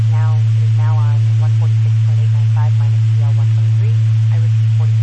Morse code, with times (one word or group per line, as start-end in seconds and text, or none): none